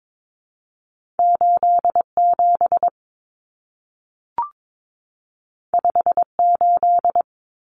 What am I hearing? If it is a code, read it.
Morse code, heard as a keyed tone, about 22 wpm, 700 Hz. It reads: 87 58
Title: Numbers - 2-Digit 22wpm